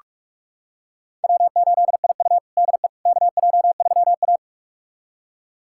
Morse code, heard as a keyed tone, about 42 wpm, 700 Hz. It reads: W8IU DE KJ3A
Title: USA Callsign DE Callsign - 4 Characters - 45wpm